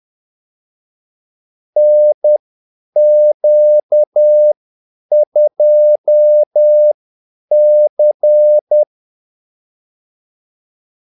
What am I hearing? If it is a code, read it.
Morse code, heard as a keyed tone, about 10 wpm, spaced as for 8 wpm, 600 Hz. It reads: NQ2C
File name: Callsigns-10-8wpm